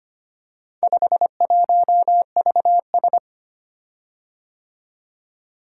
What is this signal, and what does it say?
Morse code, heard as a keyed tone, about 25 wpm, 700 Hz. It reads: 51VS
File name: practice / 25wpm / Four Letter-Number Combinations - 25wpm